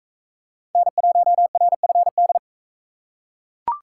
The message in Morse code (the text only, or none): N1RFD